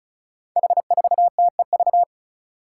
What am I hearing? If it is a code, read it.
Morse code, heard as a keyed tone, about 36 wpm, 700 Hz. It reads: H4TEV
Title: Five Challenging Letter-Number Combinations 35wpm